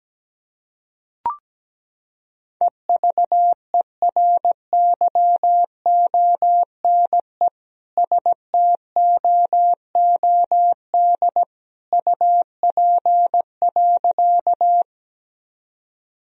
Morse code, heard as a keyed tone, about 17 wpm, 700 Hz. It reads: EVERYONE STOOD UP.
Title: Sentences from Top 1000 Words - Review Part 2 - 17wpm